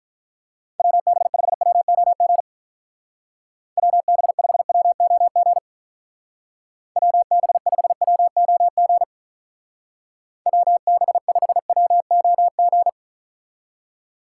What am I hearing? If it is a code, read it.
Morse code, heard as a keyed tone, about 44 wpm, 700 Hz. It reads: WB5WOG WB5WOG WB5WOG WB5WOG